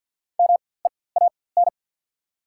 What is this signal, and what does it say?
Morse code, heard as a keyed tone, about 48 wpm, spaced as for 23 wpm, 690 Hz. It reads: MEAN
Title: 4x Character Spacing - Top 300 Words - 50wpm